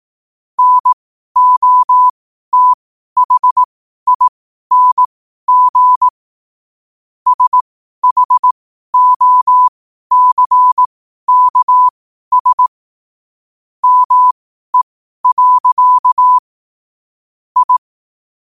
Morse code, heard as a keyed tone, about 18 wpm, spaced as for 12 wpm, 1000 Hz. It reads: NOTHING SHOCKS ME. I
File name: qotdfiles / 12wpm